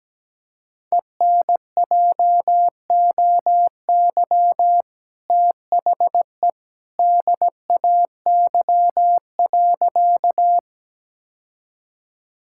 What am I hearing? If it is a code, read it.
Morse code, heard as a keyed tone, about 17 wpm, 700 Hz. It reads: ENJOY THE DAY.
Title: Sentences from Top 1000 Words - Review Part 5 - 17wpm